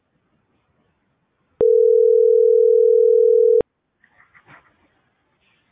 Ringback tone (North America), 440 + 480 Hz.